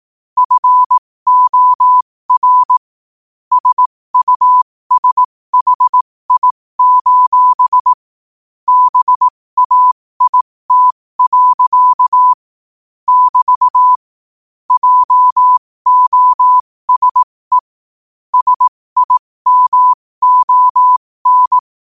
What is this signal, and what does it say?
Morse code, heard as a keyed tone, about 18 wpm, spaced as for 15 wpm, 1000 Hz. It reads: FOR SUSHI: BAIT. <BT> JOSE SIMON